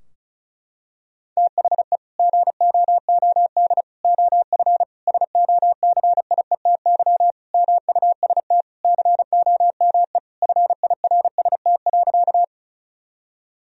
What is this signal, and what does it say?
Morse code, heard as a keyed tone, about 36 wpm, 700 Hz. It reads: THE GOOD OF SOCIETY MUST COME FIRST.
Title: Sentences from Top 1000 Words - Review Part 4 - 35wpm